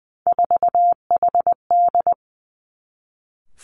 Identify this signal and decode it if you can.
Morse code, heard as a keyed tone, about 20 wpm, 700 Hz. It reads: 4HD
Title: Three Challenging Letter-Number Combinations 20wpm